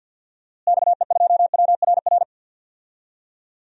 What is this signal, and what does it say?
Morse code, heard as a keyed tone, about 47 wpm, 700 Hz. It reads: XE2WRR
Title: DXCC-100 Call Signs - 6 Characters - Repeated 2x - 50wpm